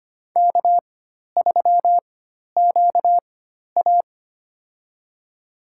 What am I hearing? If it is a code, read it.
Morse code, heard as a keyed tone, about 25 wpm, spaced as for 12 wpm, 690 Hz. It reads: K3QA
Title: USA Call Signs - 4 Characters - 4x Character Spacing - 25wpm